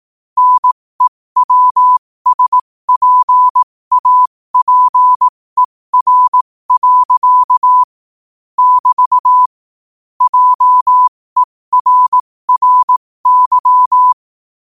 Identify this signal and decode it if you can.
Morse code, heard as a keyed tone, about 18 wpm, spaced as for 15 wpm, 1000 Hz. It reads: NEWSPAPER. <BT> JERRY
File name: qotdfiles / 15wpm